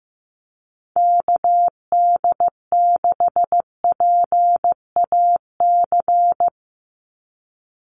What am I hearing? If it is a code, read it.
Morse code, heard as a keyed tone, about 15 wpm, 700 Hz. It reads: KD6PAC
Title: USA Call Signs 15wpm